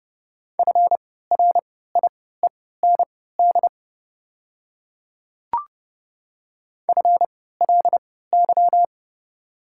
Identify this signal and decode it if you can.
Morse code, heard as a keyed tone, about 30 wpm, spaced as for 17 wpm, 690 Hz. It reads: FRIEND FLY
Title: 3x Character Spacing - Top 500 Words - 30wpm